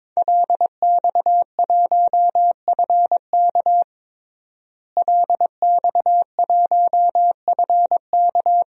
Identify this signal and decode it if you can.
Morse code, heard as a keyed tone, about 22 wpm, 700 Hz. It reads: LX1FK LX1FK
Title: DXCC-100 Call Signs - 5 Characters - Repeated 2x - 22wpm